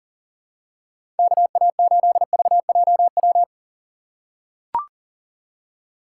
Morse code, heard as a keyed tone, about 40 wpm, 700 Hz. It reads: KA8VJW
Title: USA Call Signs - 3x Character Spacing - 40wpm